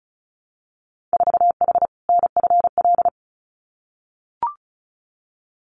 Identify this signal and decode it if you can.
Morse code, heard as a keyed tone, about 36 wpm, 700 Hz. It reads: 4H NFL
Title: Winter Field Day Exchanges Encore 35wpm